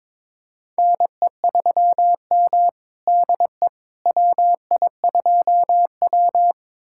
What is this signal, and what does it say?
Morse code, heard as a keyed tone, about 22 wpm, 700 Hz. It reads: NE3M DE WI2W
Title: USA Callsign DE Callsign - 4 Characters - 22wpm